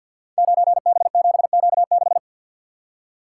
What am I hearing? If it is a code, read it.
Morse code, heard as a keyed tone, about 51 wpm, 700 Hz. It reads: 9B7Q6
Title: Code Groups - 5 Letter:Numbers - 50wpm